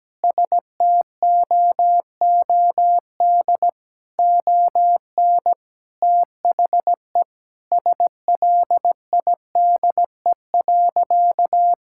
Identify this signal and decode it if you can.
Morse code, heard as a keyed tone, about 17 wpm, 700 Hz. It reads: STOOD ON THE SLIDE.
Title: Sentences from Top 1600 Words 17wpm